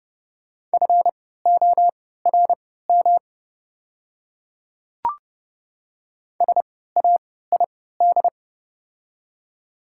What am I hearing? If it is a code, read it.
Morse code, heard as a keyed tone, about 30 wpm, spaced as for 17 wpm, 690 Hz. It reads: FORM SAID
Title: 3x Character Spacing - Top 200 Words - 30wpm